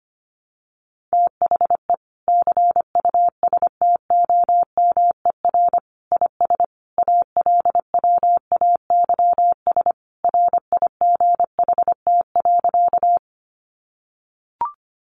Morse code, heard as a keyed tone, about 25 wpm, 700 Hz. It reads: THE CUSTOMER IS ALWAYS RIGHT.
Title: Sentences from Top 1100 Words 25wpm